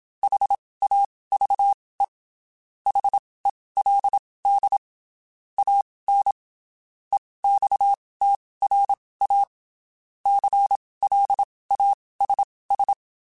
Morse code, heard as a keyed tone, about 27 wpm, spaced as for 19 wpm, 800 Hz. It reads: HAVE HELD AN EXTRA CLASS
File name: (spaced) / QSOs Spaced-mp3s / kc7haa @25 sp